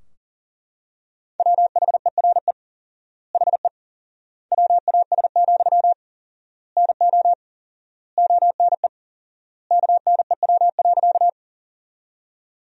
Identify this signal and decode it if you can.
Morse code, heard as a keyed tone, about 40 wpm, 700 Hz. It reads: WHERE HE WAS, NO ONE KNEW.
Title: 4x Word-Spacing - Sentences from Top 400 Words - 40wpm